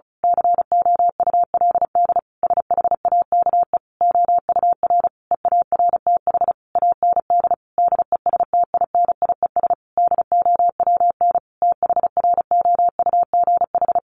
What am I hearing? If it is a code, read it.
Morse code, heard as a keyed tone, about 34 wpm, 700 Hz. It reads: COULD SHAKE OUR EARTH AND DESTINIES DOWN THROUGH
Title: A Sound Of Thunder 35wpm